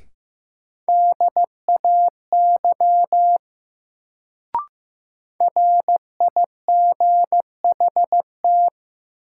Morse code, heard as a keyed tone, about 15 wpm, 700 Hz. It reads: DAY RIGHT
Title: Sets of 2 Words - New Word Drill - 15wpm